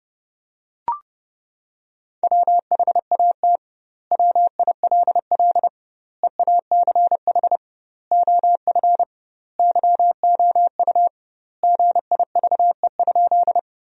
Morse code, heard as a keyed tone, about 30 wpm, 700 Hz. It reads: WHAT WILL EACH OF YOU GIVE?
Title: 2x Word-Spacing - Sentences from Top 200 Words - 30wpm